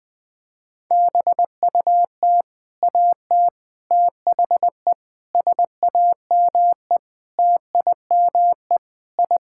Morse code, heard as a keyed tone, about 20 wpm, 700 Hz. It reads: BUT AT THE SAME TIME I
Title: Sentences with 4-Word Phrases - 20wpm